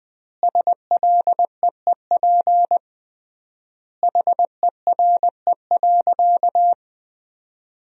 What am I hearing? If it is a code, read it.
Morse code, heard as a keyed tone, about 21 wpm, 700 Hz. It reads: SLEEP HERE.
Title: 3x Word-Spacing - Sentences from Top 500 Words - 20wpm